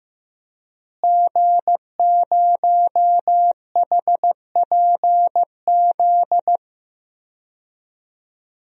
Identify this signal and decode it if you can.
Morse code, heard as a keyed tone, about 15 wpm, 700 Hz. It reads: G0HPZ
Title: DXCC-100 Call Signs - 5 Characters - Repeated 2x - 15wpm